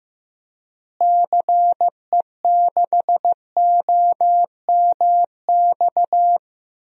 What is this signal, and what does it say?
Morse code, heard as a keyed tone, about 15 wpm, 700 Hz. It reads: CE6OMX
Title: DXCC-100 Call Signs - 6 Characters - Repeated 2x - 15wpm